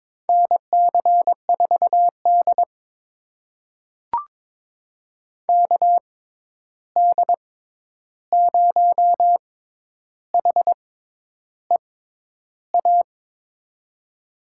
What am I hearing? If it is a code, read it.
Morse code, heard as a keyed tone, about 22 wpm, spaced as for 10 wpm, 700 Hz. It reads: NC4D KD0HEA
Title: USA Call Signs - 6x Character Spacing - 22wpm